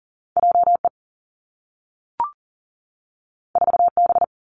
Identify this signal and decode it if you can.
Morse code, heard as a keyed tone, about 40 wpm, 700 Hz. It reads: J E 4 B
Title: Two Challenging Letter-Number Combinations 40wpm